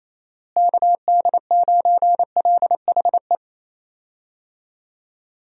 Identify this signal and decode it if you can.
Morse code, heard as a keyed tone, about 28 wpm, 700 Hz. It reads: KD9LHE
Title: USA Call Signs - 6 Characters - Repeated 2x - 28wpm